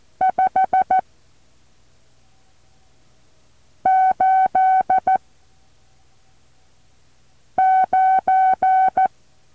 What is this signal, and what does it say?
Morse code, heard as a keyed tone, about 14 wpm, spaced as for 3 wpm, 760 Hz. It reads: 589
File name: intl morse code2